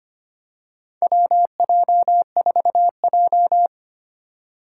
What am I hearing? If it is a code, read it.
Morse code, heard as a keyed tone, about 25 wpm, 690 Hz. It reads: WJ4J
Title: USA Call Signs - 4 Characters - 4x Character Spacing - 25wpm